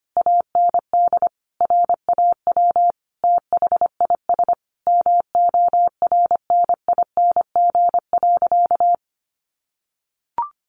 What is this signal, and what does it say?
Morse code, heard as a keyed tone, about 25 wpm, 700 Hz. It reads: AND RAW THIS MORNING.
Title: Sentences from Top 1500 Words 25wpm